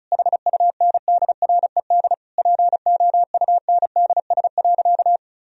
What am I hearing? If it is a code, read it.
Morse code, heard as a keyed tone, about 36 wpm, 700 Hz. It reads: HUNDRED POUNDS.
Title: Sentences from Top 1000 Words - Review Part 4 - 35wpm